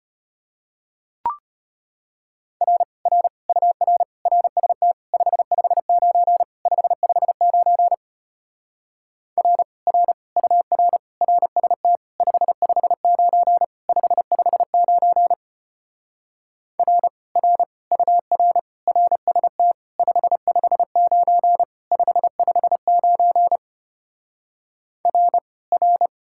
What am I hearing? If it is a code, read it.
Morse code, heard as a keyed tone, about 33 wpm, 700 Hz. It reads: R R UR RST 559 559 R R UR RST 559 559 R R UR RST 559 559 R R